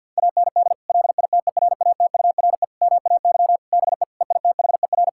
Morse code, heard as a keyed tone, about 50 wpm, 700 Hz. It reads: AND LITERATURE MAY BE EITHER
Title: Sentences from Top 1600 Words 50wpm